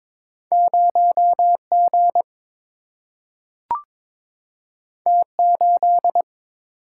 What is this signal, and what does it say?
Morse code, heard as a keyed tone, about 22 wpm, 700 Hz. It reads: 0 G T 8
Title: Single-Letter Single-Number 22wpm